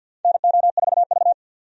Morse code, heard as a keyed tone, about 51 wpm, 700 Hz. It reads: NO4V
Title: USA Call Signs - 4 Characters - 50wpm